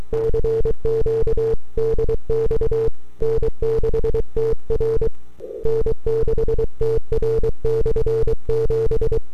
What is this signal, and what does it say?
Morse code, heard as a keyed tone, about 23 wpm, 450 Hz. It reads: CQ DX N6TR N6TR/7
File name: n6tr / 12Oct2009-160